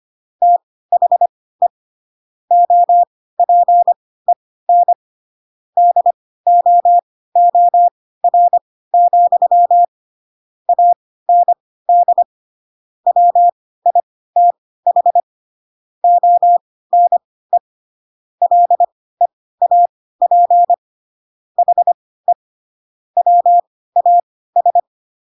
Morse code, heard as a keyed tone, about 25 wpm, spaced as for 16 wpm, 700 Hz. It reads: THE OPEN DOOR, AND WITH ONE LEAP HE WAS
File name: Pinocchio / Pinocchio ch3 10 25 ik1hge